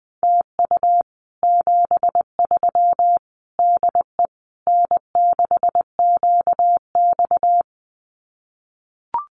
Morse code, heard as a keyed tone, about 20 wpm, 700 Hz. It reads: TU 73 DE N6QX
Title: Common QSO Phrases Encore 20wpm